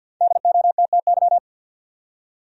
Morse code, heard as a keyed tone, about 52 wpm, 700 Hz. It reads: DOTTY
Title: USA Common Names - Max 5 Letters - 50wpm